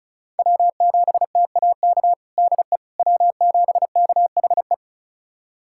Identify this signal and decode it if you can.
Morse code, heard as a keyed tone, about 36 wpm, 700 Hz. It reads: W7TAK DE W7KHE
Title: USA Callsign DE Callsign - 5 Characters - 35wpm